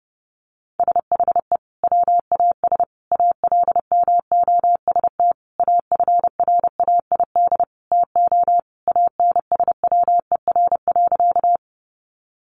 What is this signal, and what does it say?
Morse code, heard as a keyed tone, about 28 wpm, 700 Hz. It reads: SHE WAS ALMOST AFRAID TO ANSWER.
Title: Sentences from Top 1500 Words 30wpm